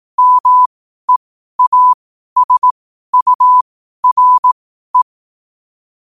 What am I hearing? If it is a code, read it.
Morse code, heard as a keyed tone, about 18 wpm, spaced as for 12 wpm, 1000 Hz. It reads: MEASURE
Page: qotd-at-12-wpm